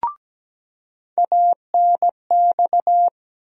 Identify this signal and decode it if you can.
Morse code, heard as a keyed tone, about 17 wpm, 700 Hz. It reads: ANX
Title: USA Call Sign Postfixes - 3 Letters 17wpm